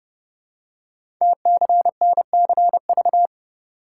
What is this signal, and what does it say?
Morse code, heard as a keyed tone, about 30 wpm, 700 Hz. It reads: TCNCV
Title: Code Groups - 5 Letters - 30wpm